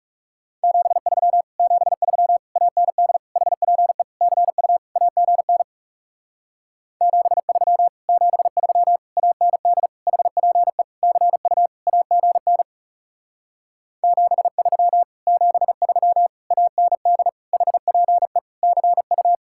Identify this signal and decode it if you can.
Morse code, heard as a keyed tone, about 40 wpm, 700 Hz. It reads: 73 73 AND HPE CU AGN 73 73 AND HPE CU AGN 73 73 AND HPE CU